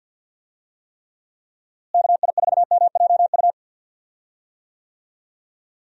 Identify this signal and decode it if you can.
Morse code, heard as a keyed tone, about 52 wpm, 700 Hz. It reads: KI4MJU
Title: USA Call Signs - 6 Characters - Repeated 3x - 50wpm